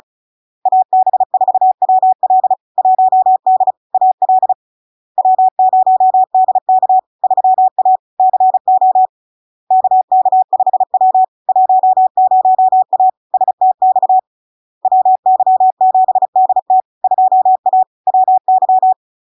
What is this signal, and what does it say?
Morse code, heard as a keyed tone, about 35 wpm, 750 Hz. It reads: AB4WL 1D AL W0DK 3A CO KK5W 10A STX WY7DT 2A WY
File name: LARK-FD-2022-Log-in-CW-35wpm-750-Hz